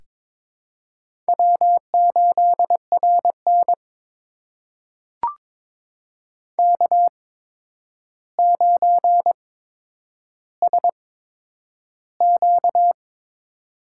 Morse code, heard as a keyed tone, about 22 wpm, spaced as for 9 wpm, 700 Hz. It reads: W8RN K9SQ
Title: USA Call Signs - 4 Characters - 8x Character Spacing - 22wpm